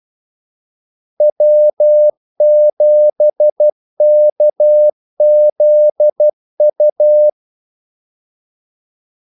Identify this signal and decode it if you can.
Morse code, heard as a keyed tone, about 12 wpm, 600 Hz. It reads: W7KZU
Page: 2 ekstra ordmellemrum mellem kaldesignalerne